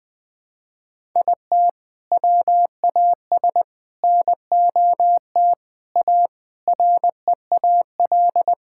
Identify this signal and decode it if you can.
Morse code, heard as a keyed tone, about 20 wpm, 700 Hz. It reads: IT WAS NOT A REAL
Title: Sentences from Top 1000 Words - Review Part 6 - 20wpm